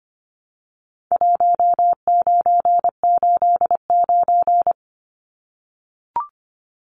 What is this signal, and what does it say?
Morse code, heard as a keyed tone, about 25 wpm, 700 Hz. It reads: 1989
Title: Numbers - 4-Digit 25wpm